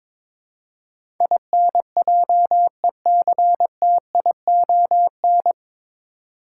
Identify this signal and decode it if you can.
Morse code, heard as a keyed tone, about 22 wpm, 700 Hz. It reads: INJECTION
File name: 9 Letter Words - First 3 Letters Given - 22wpm